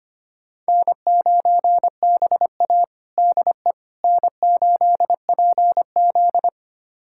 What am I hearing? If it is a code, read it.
Morse code, heard as a keyed tone, about 25 wpm, 700 Hz. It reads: N9BA DE N8PZ
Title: USA Callsign DE Callsign - 4 Characters - 25wpm